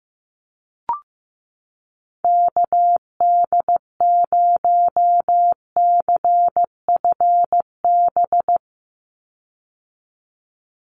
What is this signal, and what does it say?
Morse code, heard as a keyed tone, about 15 wpm, 700 Hz. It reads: KD0CFB
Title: USA Call Signs 15wpm